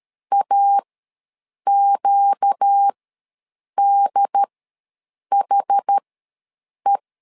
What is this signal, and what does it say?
Morse code, heard as a keyed tone, about 13 wpm, spaced as for 7 wpm, 800 Hz. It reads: AQDHE
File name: LEZIONI / 4 Q ws 12wpm 1912